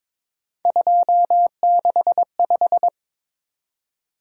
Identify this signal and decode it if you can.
Morse code, heard as a keyed tone, about 25 wpm, 700 Hz. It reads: 265
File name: Speed-Racing - Three Letter-Number Combinations - 22wpm